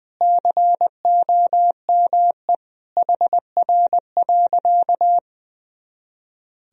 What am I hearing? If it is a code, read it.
Morse code, heard as a keyed tone, about 21 wpm, 700 Hz. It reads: COME HR.
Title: Sentences from Top 300 Words and Abbreviations 20wpm